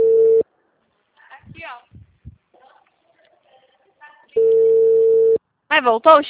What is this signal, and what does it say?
Ringback tone (China, 450 Hz), with speech.